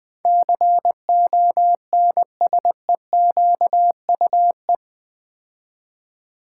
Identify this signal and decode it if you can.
Morse code, heard as a keyed tone, about 20 wpm, 700 Hz. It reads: CONSEQUE
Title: Build A Word - 4 Letters At A Time - 12-Letter Words - 20wpm